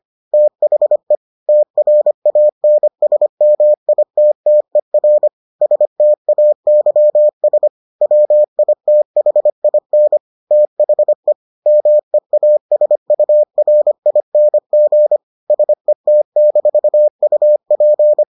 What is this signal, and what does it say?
Morse code, heard as a keyed tone, about 25 wpm, 600 Hz. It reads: THE TRANSMITTER STAYS WITHIN THE MEASURING SET-UP